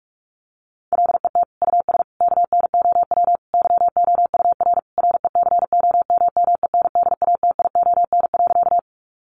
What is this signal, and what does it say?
Morse code, heard as a keyed tone, about 44 wpm, 700 Hz. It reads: LET US KNOW YOUR RECOMMENDATION.
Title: Sentences from Top 1400 Words 45wpm